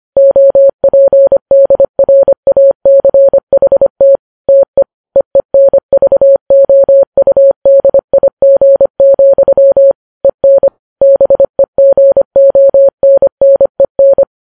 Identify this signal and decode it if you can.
Morse code, heard as a keyed tone, about 25 wpm, 550 Hz. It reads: OPDRACHT TE EENVOUDIG, EN BEGONNEN